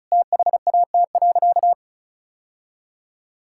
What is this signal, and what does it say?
Morse code, heard as a keyed tone, about 34 wpm, 700 Hz. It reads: THAT.
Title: Sentences from Top 1000 Words - Review Part 6 - 35wpm